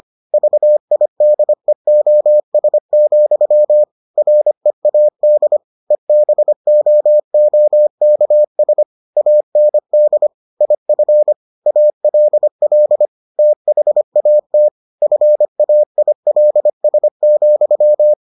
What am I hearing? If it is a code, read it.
Morse code, heard as a keyed tone, about 25 wpm, 600 Hz. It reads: VIDEOS, READ EBOOKS AND IF ALL THAT FAILS,